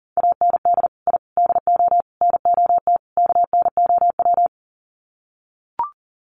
Morse code, heard as a keyed tone, about 40 wpm, 700 Hz. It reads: AND I DO NOT KNOW
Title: Sets of 5 Words 40wpm